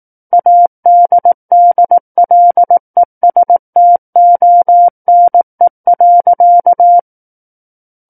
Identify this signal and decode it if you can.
Morse code, heard as a keyed tone, about 18 wpm, 700 Hz. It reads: ADDLESTONE.
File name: 18 WPM